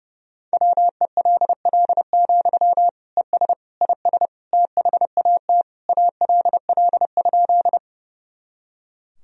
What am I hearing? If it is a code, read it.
Morse code, heard as a keyed tone, about 30 wpm, 700 Hz. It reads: WELL, ES IS THAT ALL?
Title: Sentences from Top 200 Words and Abbreviations 30wpm